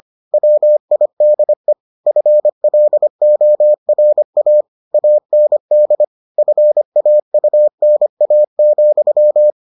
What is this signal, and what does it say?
Morse code, heard as a keyed tone, about 25 wpm, 600 Hz. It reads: WIDE FLORA AND FAUNA,